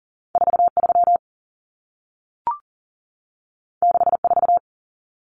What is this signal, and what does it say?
Morse code, heard as a keyed tone, about 40 wpm, 700 Hz. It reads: 43 64
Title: Numbers - 2-Digit 40wpm